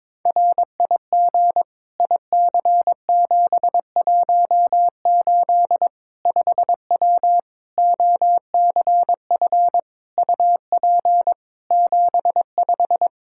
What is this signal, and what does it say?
Morse code, heard as a keyed tone, about 22 wpm, 700 Hz. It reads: RIG IC718 5W OCF UP 75
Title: K1USN SST QSO Practice 22wpm